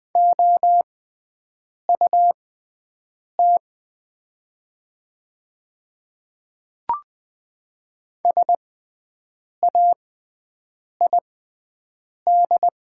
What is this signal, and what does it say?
Morse code, heard as a keyed tone, about 20 wpm, spaced as for 7 wpm, 700 Hz. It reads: OUT SAID
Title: 6x Character Spacing - Top 100 Words - 20wpm